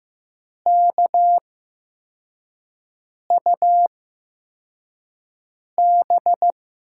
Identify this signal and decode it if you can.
Morse code, heard as a keyed tone, about 15 wpm, spaced as for 4 wpm, 700 Hz. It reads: KUB
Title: USA Call Signs - 6 Characters - 8x Character Spacing - 15wpm